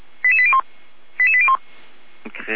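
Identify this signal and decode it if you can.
Five-tone selective call (ZVEI), heard as quick runs of five tones, tones of about 70 ms. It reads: 88801 88801
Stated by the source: ZVEI five-tone selective call received by SDR.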